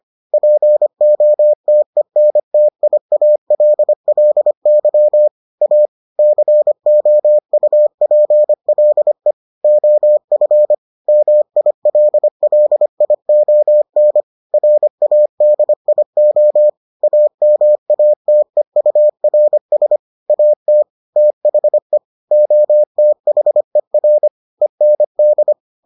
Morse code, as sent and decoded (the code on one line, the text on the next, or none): .--. --- - . -. - .. .- .-.. .-.. -.-- / .- / -.-. --- ..- .--. .-.. / --- ..-. / -- .. .-.. .-.. .. --- -. / .-. .- -.. .. --- / .- -- .- - . ..- .-. ... / .- - / - .... / --- - .... .-. / . -. -..
POTENTIALLY A COUPLE OF MILLION RADIO AMATEURS AT THE OTHER END